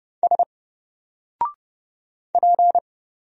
Morse code, heard as a keyed tone, about 30 wpm, 700 Hz. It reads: S P
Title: Single Letter-Number Alt 30wpm